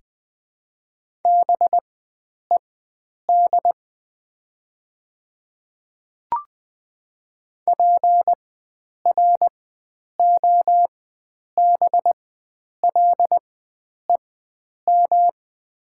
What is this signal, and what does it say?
Morse code, heard as a keyed tone, about 20 wpm, spaced as for 9 wpm, 700 Hz. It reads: BED PROBLEM
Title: 4x Character Spacing - Top 500 Words - 20wpm